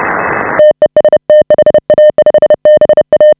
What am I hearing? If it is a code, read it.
Morse code, heard as a keyed tone, about 30 wpm, 600 Hz. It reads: TEST HA5BA
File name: AUDIO2019 / HA5BA CW